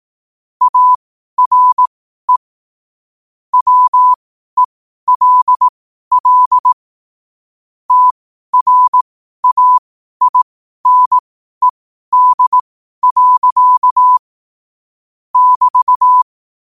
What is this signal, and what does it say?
Morse code, heard as a keyed tone, about 18 wpm, spaced as for 12 wpm, 1000 Hz. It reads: ARE WELL TRAINED. <BT>